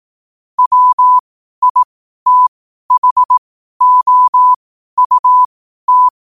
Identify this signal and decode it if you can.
Morse code, heard as a keyed tone, about 18 wpm, spaced as for 12 wpm, 1000 Hz. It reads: WITHOUT
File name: qotdfiles / 12wpm